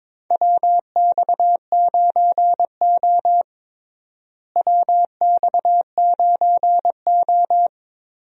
Morse code, heard as a keyed tone, about 22 wpm, 700 Hz. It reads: WX9O WX9O
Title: USA Call Signs - 4 Characters - Repeated 2x - 22wpm